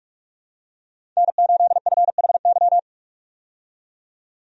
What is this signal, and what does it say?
Morse code, heard as a keyed tone, about 46 wpm, 700 Hz. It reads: N8FHY
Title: USA Call Signs - 5 Characters - 45wpm